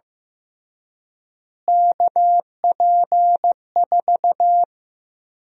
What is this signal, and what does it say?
Morse code, heard as a keyed tone, about 15 wpm, 700 Hz. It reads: KP4
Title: USA Call Sign Prefixes - Repeated 2x - 15wpm